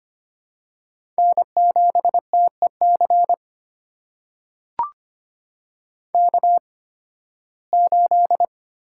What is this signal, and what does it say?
Morse code, heard as a keyed tone, about 25 wpm, spaced as for 16 wpm, 700 Hz. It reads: N7TEC K8
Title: USA Call Signs - 5 Characters - 8x Character Spacing - 25wpm